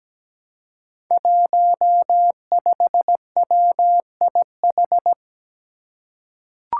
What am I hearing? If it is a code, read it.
Morse code, heard as a keyed tone, about 17 wpm, 700 Hz. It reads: 15WIH
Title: Five Challenging Letter-Number Combinations 17wpm